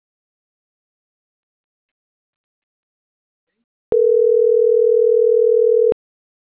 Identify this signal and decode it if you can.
Ringback tone (North America), 440 + 480 Hz.